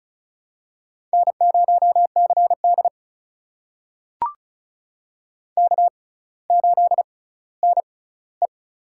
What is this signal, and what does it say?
Morse code, heard as a keyed tone, about 34 wpm, spaced as for 18 wpm, 700 Hz. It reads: N0CD K8NE
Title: USA Call Signs - 4 Characters - 6x Character Spacing - 35wpm